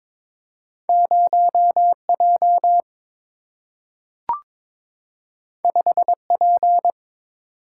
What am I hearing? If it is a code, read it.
Morse code, heard as a keyed tone, about 22 wpm, 700 Hz. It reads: 0 J 5 P
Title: Single-Letter Single-Number 22wpm